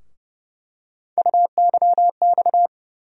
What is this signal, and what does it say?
Morse code, heard as a keyed tone, about 30 wpm, 700 Hz. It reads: UYX
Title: USA Call Sign Postfixes - 3 Letters 30wpm